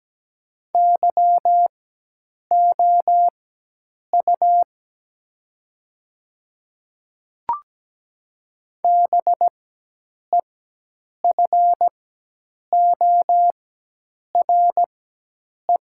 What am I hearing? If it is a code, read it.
Morse code, heard as a keyed tone, about 17 wpm, spaced as for 8 wpm, 700 Hz. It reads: YOU BEFORE
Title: 4x Character Spacing - Top 200 Words - 17wpm